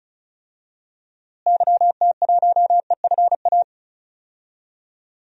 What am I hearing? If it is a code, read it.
Morse code, heard as a keyed tone, about 34 wpm, 700 Hz. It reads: YT1EFA
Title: DXCC-100 Call Signs - 6 Characters - Repeated 2x - 35wpm